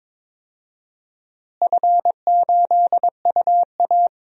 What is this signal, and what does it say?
Morse code, heard as a keyed tone, about 21 wpm, 700 Hz. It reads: F8UA
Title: DXCC-100 Call Signs - 4 Characters - Repeated 3x - 22wpm